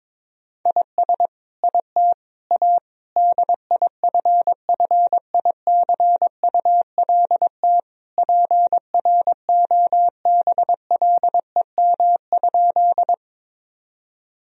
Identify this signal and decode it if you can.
Morse code, heard as a keyed tone, about 22 wpm, 700 Hz. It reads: IS IT A DIFFICULT PROBLEM?
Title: Sentences from Top 1000 Words - Review Part 2 - 22wpm